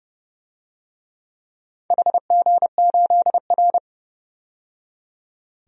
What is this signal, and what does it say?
Morse code, heard as a keyed tone, about 30 wpm, 700 Hz. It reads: HG8R
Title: DXCC-100 Call Signs - 4 Characters - 30wpm